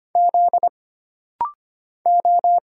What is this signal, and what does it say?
Morse code, heard as a keyed tone, about 25 wpm, 700 Hz. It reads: Z O
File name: Single Character (Letter:Number:Punctuation:Prosign) - Rapid-Fire - 25wpm